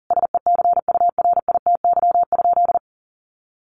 Morse code, heard as a keyed tone, about 40 wpm, 700 Hz. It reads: SECURITY?
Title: Sentences from Top 1400 Words 40wpm